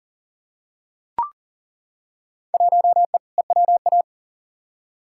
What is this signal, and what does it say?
Morse code, heard as a keyed tone, about 40 wpm, 700 Hz. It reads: 1E EWA
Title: Field Day Exchanges - 40wpm